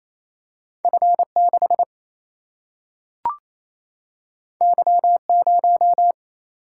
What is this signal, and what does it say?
Morse code, heard as a keyed tone, about 28 wpm, 700 Hz. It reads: F6 Y0
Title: Single-Letter Single-Number 28wpm